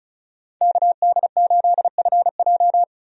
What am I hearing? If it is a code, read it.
Morse code, heard as a keyed tone, about 36 wpm, 700 Hz. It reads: KD8FJ
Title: USA Call Signs - 5 Characters - 35wpm